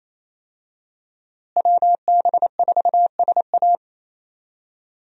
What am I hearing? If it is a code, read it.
Morse code, heard as a keyed tone, about 28 wpm, 700 Hz. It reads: WB4SA
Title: USA Call Signs - 5 Characters - Repeated 2x - 28wpm